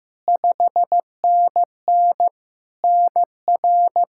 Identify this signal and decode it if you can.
Morse code, heard as a keyed tone, about 15 wpm, 700 Hz. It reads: 5NN NR
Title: Practice QSO - Exchange 1 - Beginning - 15wpm